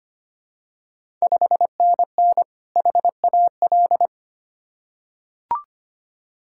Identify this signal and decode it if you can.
Morse code, heard as a keyed tone, about 25 wpm, 700 Hz. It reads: 5NN HAL
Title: Ontario QSO Party Exchange 25wpm